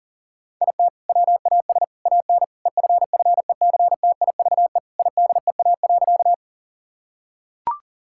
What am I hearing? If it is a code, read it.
Morse code, heard as a keyed tone, about 40 wpm, 700 Hz. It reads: IT WAS AN EFFECTIVE IDEA.
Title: Sentences from Top 1000 Words - Review Part 6 - 40wpm